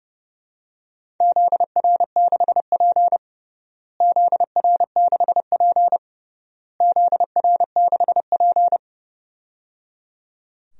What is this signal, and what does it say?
Morse code, heard as a keyed tone, about 30 wpm, 700 Hz. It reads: ZR6P ZR6P ZR6P
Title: DXCC-100 Call Signs - 4 Characters - Repeated 3x - 30wpm